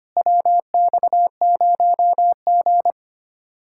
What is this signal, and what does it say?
Morse code, heard as a keyed tone, about 32 wpm, 700 Hz. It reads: WX0G